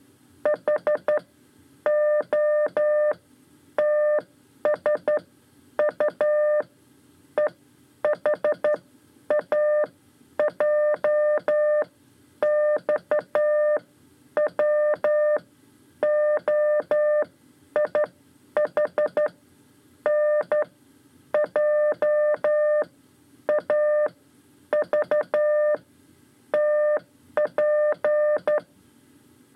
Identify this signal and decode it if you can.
Morse code, heard as a keyed tone, about 12 wpm, spaced as for 8 wpm, 600 Hz. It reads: HOTSU EHAJX WOIHN JAVTP
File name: 5 Lezione CW 2